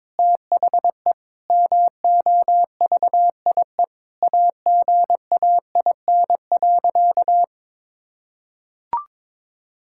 Morse code, heard as a keyed tone, about 21 wpm, 700 Hz. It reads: THE MOVIE AGAIN.
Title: Sentences from Top 1000 Words - Review Part 5 - 22wpm